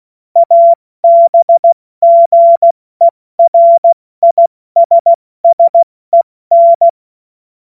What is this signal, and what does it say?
Morse code, heard as a keyed tone, about 16 wpm, spaced as for 14 wpm, 680 Hz. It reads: ABGERISSEN
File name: Morsekurs / Witze / Witz 02. August 2022 70 F80